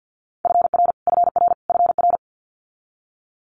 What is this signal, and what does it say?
Morse code, heard as a keyed tone, about 50 wpm, 700 Hz. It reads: FR FR FR
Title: Multi-letter Phonemes Repeated 3x 50wpm